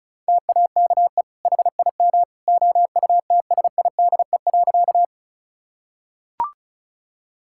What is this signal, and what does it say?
Morse code, heard as a keyed tone, about 34 wpm, 700 Hz. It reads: TAKE HIM OUTSIDE.
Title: Sentences from Top 1000 Words - Review Part 6 - 35wpm